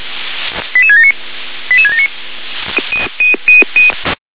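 Five-tone selective call (ZVEI), heard as quick runs of five tones, tones of about 70 ms. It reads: 88679 88679
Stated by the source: ZVEI five-tone selective call received on SDR.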